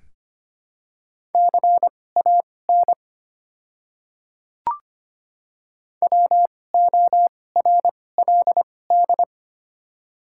Morse code, heard as a keyed tone, about 25 wpm, spaced as for 18 wpm, 700 Hz. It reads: CAN WORLD
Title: 2x Character Spacing - Top 300 Words - 25wpm